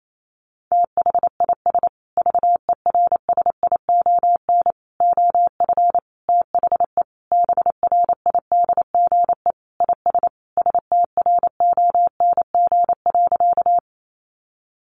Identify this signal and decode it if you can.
Morse code, heard as a keyed tone, about 28 wpm, 700 Hz. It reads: THIS VERSION OF THE BRIDGE IS STRONG.
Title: Sentences from Top 1400 Words 28wpm